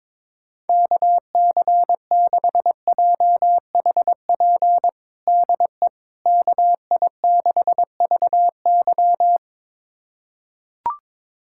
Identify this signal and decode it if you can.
Morse code, heard as a keyed tone, about 22 wpm, 700 Hz. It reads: KC6JHP DE KI6VY
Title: USA Callsign DE Callsign 22wpm